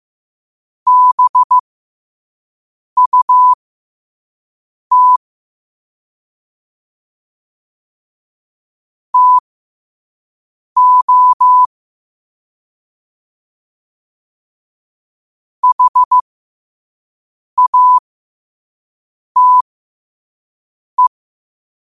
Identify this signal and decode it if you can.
Morse code, heard as a keyed tone, about 15 wpm, spaced as for 5 wpm, 1000 Hz. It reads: BUT TO HATE
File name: qotdfiles / 5wpm